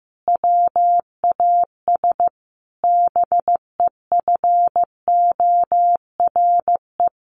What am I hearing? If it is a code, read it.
Morse code, heard as a keyed tone, about 15 wpm, 700 Hz. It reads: WAS BEFORE
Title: Sentences from Top 1300 Words 15wpm